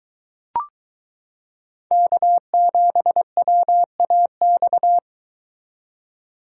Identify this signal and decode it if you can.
Morse code, heard as a keyed tone, about 22 wpm, 700 Hz. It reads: K7WAX